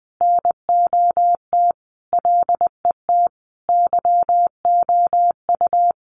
Morse code, heard as a keyed tone, about 20 wpm, 700 Hz. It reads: NOT LET YOU
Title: Sentences from Top 1500 Words 20wpm